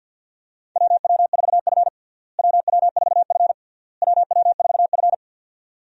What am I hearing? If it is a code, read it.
Morse code, heard as a keyed tone, about 53 wpm, 700 Hz. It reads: WW4F WW4F WW4F
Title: USA Call Signs - 4 Characters - Repeated 3x - 50wpm